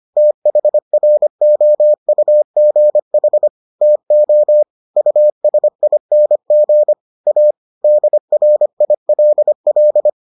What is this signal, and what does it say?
Morse code, heard as a keyed tone, about 25 wpm, 600 Hz. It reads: THROUGH TO USING A DRILL